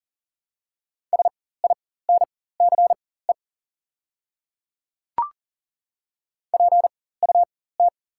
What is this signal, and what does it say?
Morse code, heard as a keyed tone, about 40 wpm, spaced as for 19 wpm, 700 Hz. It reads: SINCE PUT
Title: 4x Character Spacing - Top 500 Words - 40wpm